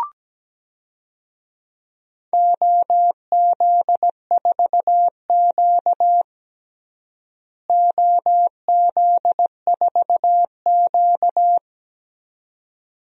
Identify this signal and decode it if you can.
Morse code, heard as a keyed tone, about 17 wpm, 700 Hz. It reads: OZ4Q OZ4Q
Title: DXCC-100 Call Signs - 4 Characters - Repeated 2x - 17wpm